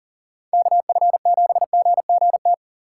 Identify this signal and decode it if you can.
Morse code, heard as a keyed tone, about 49 wpm, 700 Hz. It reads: KF7GGT
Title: Speed Racing - USA 2x3 Call Signs 40wpm